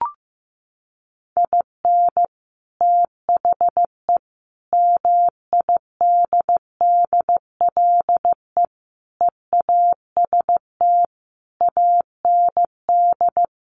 Morse code, heard as a keyed tone, about 15 wpm, 700 Hz. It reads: IN THE MIDDLE EAST AND
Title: Sets of 5 Words Encore 15wpm